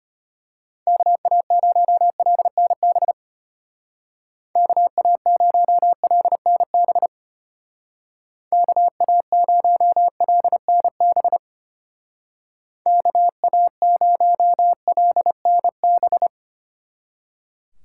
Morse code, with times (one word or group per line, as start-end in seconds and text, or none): 0.9-3.1 KA0LNB
4.6-7.1 KA0LNB
8.5-11.4 KA0LNB
12.9-16.3 KA0LNB